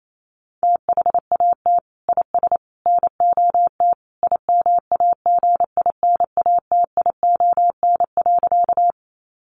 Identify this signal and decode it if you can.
Morse code, heard as a keyed tone, about 28 wpm, 700 Hz. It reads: THAT IS NOT IMAGINATION.
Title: Sentences from Top 1300 Words 28wpm